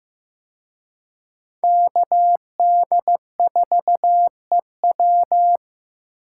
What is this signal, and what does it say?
Morse code, heard as a keyed tone, about 18 wpm, 700 Hz. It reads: KD4EW
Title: Speed Racing - USA 2x2 Call Signs 15wpm